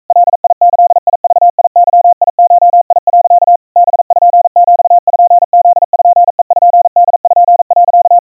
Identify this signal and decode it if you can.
Morse code (Japanese Wabun code), heard as a keyed tone, about 42 wpm, 700 Hz. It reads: ガギグゲゴ、 パピプペポ、
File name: kawa / Morse sample / iroha 140 cpm 700 Hz Crispy